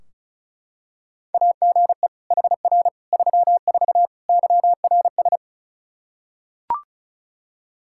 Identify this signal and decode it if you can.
Morse code, heard as a keyed tone, about 34 wpm, 700 Hz. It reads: AGE HR 34 YRS
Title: K1USN SST QSO Practice 35wpm